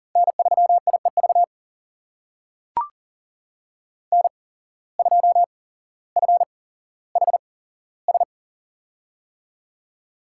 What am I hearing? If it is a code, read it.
Morse code, heard as a keyed tone, about 37 wpm, spaced as for 17 wpm, 700 Hz. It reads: N3IEV N2FHS
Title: USA Call Signs - 5 Characters - 8x Character Spacing - 40wpm